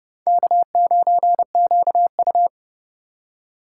Morse code, heard as a keyed tone, about 29 wpm, 700 Hz. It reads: K9QU